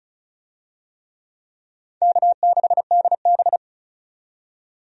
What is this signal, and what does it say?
Morse code, heard as a keyed tone, about 36 wpm, 700 Hz. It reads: K6DB
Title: USA Call Signs - 4 Characters - 35wpm